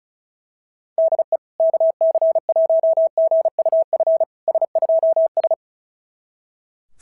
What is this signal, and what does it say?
Morse code, heard as a keyed tone, about 36 wpm, 650 Hz. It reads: DE KC1GUF S2S
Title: SOTA QSO Practice 35wpm